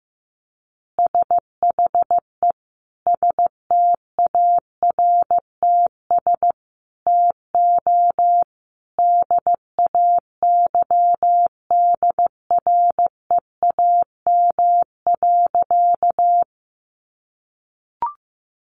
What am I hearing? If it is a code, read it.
Morse code, heard as a keyed tone, about 15 wpm, 700 Hz. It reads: SHE STARTS TO DAYDREAM.
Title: Easy Phrases 15wpm